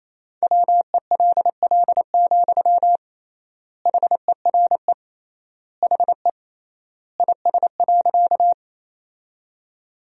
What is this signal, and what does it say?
Morse code, heard as a keyed tone, about 27 wpm, 700 Hz. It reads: WELL, HERE HE IS.
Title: 3x Word-Spacing - Sentences from Top 200 Words - 28wpm